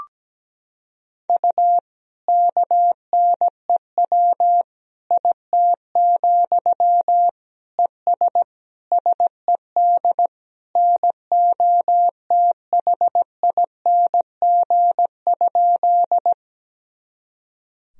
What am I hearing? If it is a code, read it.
Morse code, heard as a keyed tone, about 17 wpm, 700 Hz. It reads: U KNEW IT, ES SED NOTHING?
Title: Sentences from Top 500 Words and Abbreviations 17wpm